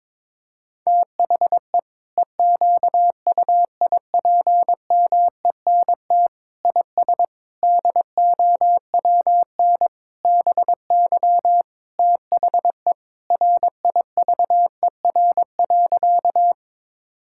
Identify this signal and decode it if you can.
Morse code, heard as a keyed tone, about 22 wpm, 700 Hz. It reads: THE EQUIPMENT IS DOWN BY THE RIVER.
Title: Sentences from Top 1000 Words - Review Part 5 - 22wpm